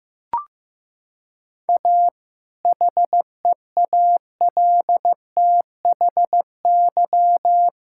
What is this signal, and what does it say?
Morse code, heard as a keyed tone, about 16 wpm, 700 Hz. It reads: A HEALTHY
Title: Sentences from Top 1000 Words - Review Part 5 - 15wpm